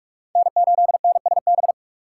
Morse code, heard as a keyed tone, about 46 wpm, 700 Hz. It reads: N8NSB
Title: USA Call Signs - 5 Characters - 4x Character Spacing - 45wpm